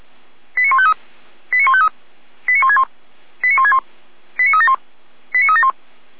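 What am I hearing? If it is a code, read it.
Five-tone selective call (ZVEI), heard as quick runs of five tones, tones of about 80 ms. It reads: 89153 89153 89161 89161 89371 89371
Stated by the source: ZVEI five-tone selective call, code 89160, received on an SDR by the Feuerwehr Orlamünde fire brigade.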